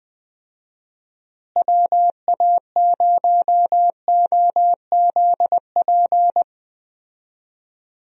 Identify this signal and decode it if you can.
Morse code, heard as a keyed tone, about 21 wpm, 700 Hz. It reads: WA0OZP
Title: USA Call Signs - 6 Characters - Repeated 2x - 20wpm